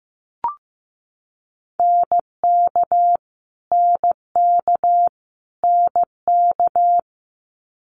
Morse code, heard as a keyed tone, about 15 wpm, 700 Hz. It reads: NK NK NK
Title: Multi-letter Phonemes Repeated 3x 15wpm